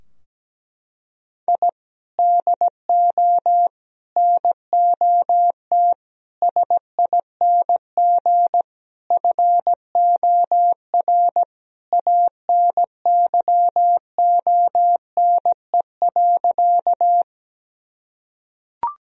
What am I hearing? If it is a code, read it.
Morse code, heard as a keyed tone, about 17 wpm, 700 Hz. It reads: I DO NOT SING FOR ANYONE.
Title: Sentences from Top 1000 Words - Review Part 2 - 17wpm